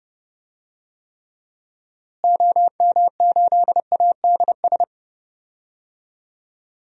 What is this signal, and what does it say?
Morse code, heard as a keyed tone, about 30 wpm, 700 Hz. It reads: OM8ADS
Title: DXCC-100 Call Signs - 6 Characters - Repeated 3x - 30wpm